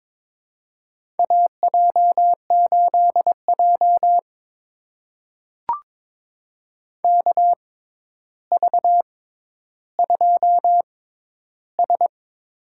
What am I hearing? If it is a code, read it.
Morse code, heard as a keyed tone, about 21 wpm, spaced as for 11 wpm, 700 Hz. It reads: AJ8J KV2S
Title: USA Call Signs - 4 Characters - 6x Character Spacing - 22wpm